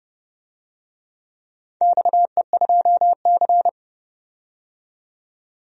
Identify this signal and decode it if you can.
Morse code, heard as a keyed tone, about 31 wpm, 700 Hz. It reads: XE2C
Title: DXCC-100 Call Signs - 4 Characters - 30wpm